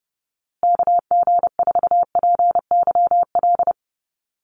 Morse code, heard as a keyed tone, about 30 wpm, 700 Hz. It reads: KG4PYL
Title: USA Call Signs Repeated 2x 30wpm